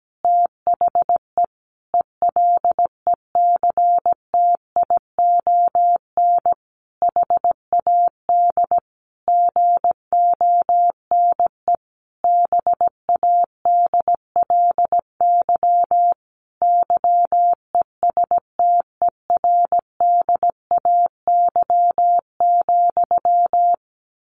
Morse code, heard as a keyed tone, about 17 wpm, 700 Hz. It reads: THE ELECTION HAD GONE BADLY YESTERDAY,
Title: A Sound Of Thunder 17wpm